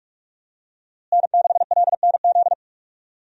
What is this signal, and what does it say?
Morse code, heard as a keyed tone, about 41 wpm, 700 Hz. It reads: N6LNZ
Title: USA Call Signs - 5 Characters - Repeated 2x - 45wpm